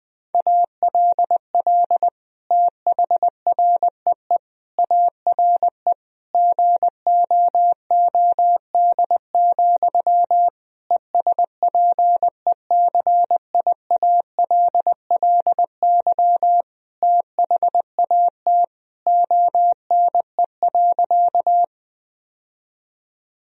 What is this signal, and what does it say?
Morse code, heard as a keyed tone, about 21 wpm, 700 Hz. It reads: ALL THREE ARE GOOD, ESPECIALLY THAT ONE.
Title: Sentences from Top 1000 Words - Review Part 4 - 20wpm